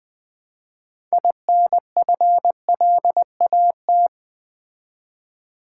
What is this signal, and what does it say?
Morse code, heard as a keyed tone, about 20 wpm, 700 Hz. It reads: INFLAT
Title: Build A Word - 3 Letters At A Time - 9-Letter Words - 20wpm